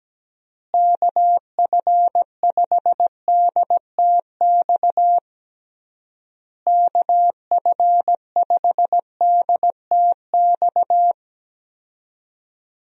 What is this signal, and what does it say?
Morse code, heard as a keyed tone, about 17 wpm, 700 Hz. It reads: KF5DTX KF5DTX
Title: USA Call Signs - 6 Characters - Repeated 3x - 17wpm